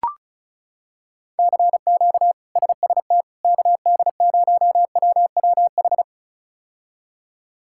Morse code, heard as a keyed tone, about 34 wpm, 700 Hz. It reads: CQ SST KD0WWH
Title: K1USN SST QSO Practice - 35wpm